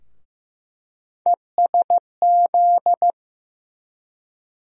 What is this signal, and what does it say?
Morse code, heard as a keyed tone, about 15 wpm, 700 Hz. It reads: ESZ
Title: USA Call Sign Postfixes - 3 Letters 15wpm